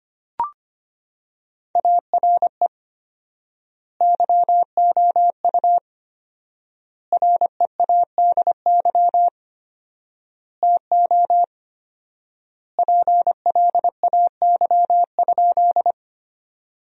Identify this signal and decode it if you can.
Morse code, heard as a keyed tone, about 25 wpm, 700 Hz. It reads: ARE YOU READY TO PLAY?
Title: 4x Word-Spacing - Sentences from Top 400 Words - 25wpm